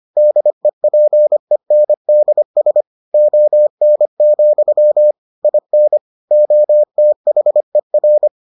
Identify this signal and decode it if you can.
Morse code, heard as a keyed tone, about 25 wpm, 600 Hz. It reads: DEPENDS ON, IN OTHER